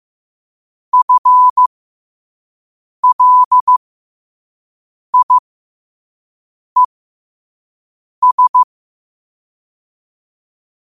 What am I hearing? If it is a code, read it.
Morse code, heard as a keyed tone, about 15 wpm, spaced as for 5 wpm, 1000 Hz. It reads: FLIES
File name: qotdfiles / 5wpm